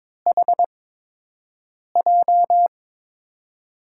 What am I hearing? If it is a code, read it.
Morse code, heard as a keyed tone, about 22 wpm, 700 Hz. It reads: H J
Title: USA Call Signs - 6 Characters - 8x Character Spacing - 22wpm